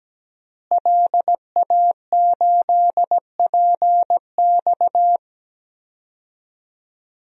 Morse code, heard as a keyed tone, about 17 wpm, 700 Hz. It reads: LA8PX
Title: DXCC-100 Call Signs - 5 Characters - Repeated 3x - 17wpm